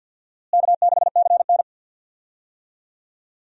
Morse code, heard as a keyed tone, about 51 wpm, 700 Hz. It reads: K6CN
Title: USA Call Signs - 4 Characters - Repeated 3x - 50wpm